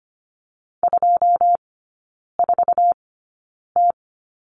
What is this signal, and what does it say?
Morse code, heard as a keyed tone, about 25 wpm, spaced as for 9 wpm, 700 Hz. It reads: 24T
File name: course / 25:10wpm / 093 Lesson 16e - Letter c - Three Characters 25wpm